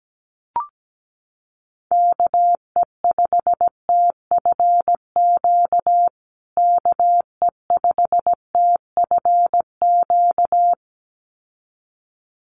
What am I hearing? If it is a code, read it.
Morse code, heard as a keyed tone, about 17 wpm, 700 Hz. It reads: KE5TFQ KE5TFQ
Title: USA Call Signs Repeated 2x 17wpm